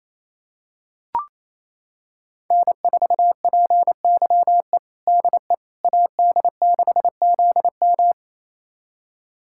Morse code, heard as a keyed tone, about 28 wpm, 700 Hz. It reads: N4PYE DE AD6ZM
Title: USA Callsign DE Callsign - 5 Characters - 28wpm